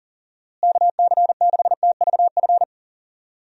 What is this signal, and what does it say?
Morse code, heard as a keyed tone, about 40 wpm, 700 Hz. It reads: KC6TVF
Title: USA Call Signs - 6 Characters - Repeated 2x - 40wpm